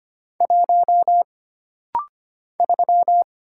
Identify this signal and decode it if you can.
Morse code, heard as a keyed tone, about 25 wpm, 700 Hz. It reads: 1 3
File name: Single Character (Letter:Number:Punctuation:Prosign) - Rapid-Fire - 25wpm